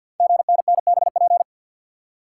Morse code, heard as a keyed tone, about 51 wpm, 700 Hz. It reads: GNNBP
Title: Code Groups - 5 Letters - 50wpm